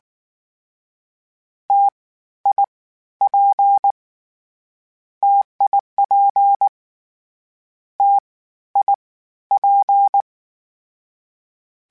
Morse code, heard as a keyed tone, about 19 wpm, spaced as for 13 wpm, 800 Hz. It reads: TIP TIP TIP
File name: Variable Speed / Variable Speed-mp3s / T AOEI's sp @18